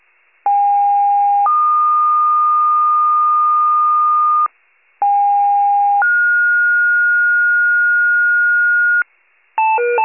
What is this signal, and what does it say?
Two-tone page, heard as a one-second tone, then a three-second tone, about 810 then 1220 Hz, about 810 then 1430 Hz, with noise. It is heard from 0.5 to 9.0 s.
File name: pager / 46 FIRE / wcalert